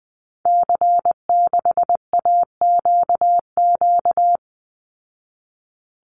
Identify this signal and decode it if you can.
Morse code, heard as a keyed tone, about 20 wpm, 700 Hz. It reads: C6AQQ
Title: Call Signs Repeated 2x - 20wpm